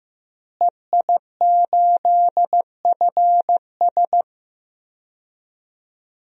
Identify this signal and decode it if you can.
Morse code, heard as a keyed tone, about 15 wpm, 700 Hz. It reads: EI8FS
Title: DXCC-100 Call Signs - 5 Characters - Repeated 3x - 15wpm